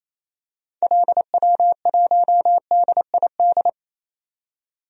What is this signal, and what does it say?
Morse code, heard as a keyed tone, about 28 wpm, 700 Hz. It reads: LW1DID
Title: DXCC-100 Call Signs - 6 Characters - 28wpm